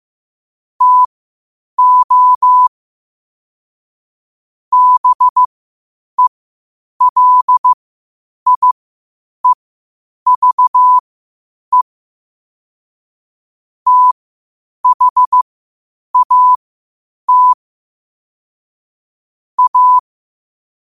Morse code, heard as a keyed tone, about 15 wpm, spaced as for 8 wpm, 1000 Hz. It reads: TO BELIEVE THAT A
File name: qotdfiles / 8wpm